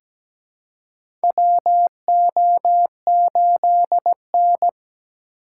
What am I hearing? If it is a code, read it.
Morse code, heard as a keyed tone, about 17 wpm, 700 Hz. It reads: WO8N
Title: USA Call Signs - 4 Characters - 3x Character Spacing - 17wpm